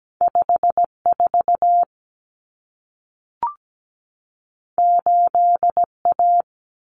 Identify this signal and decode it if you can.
Morse code, heard as a keyed tone, about 17 wpm, 700 Hz. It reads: 54 8A
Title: Common Cut Numbers - 2-Digit 17wpm